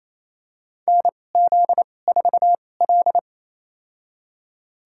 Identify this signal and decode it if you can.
Morse code, heard as a keyed tone, about 28 wpm, spaced as for 20 wpm, 700 Hz. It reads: NZ4L
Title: USA Call Signs - 4 Characters - 2x Character Spacing - 28wpm